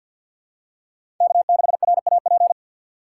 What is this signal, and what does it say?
Morse code, heard as a keyed tone, about 51 wpm, 700 Hz. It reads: K6RAP
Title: USA Call Signs - 5 Characters - 4x Character Spacing - 50wpm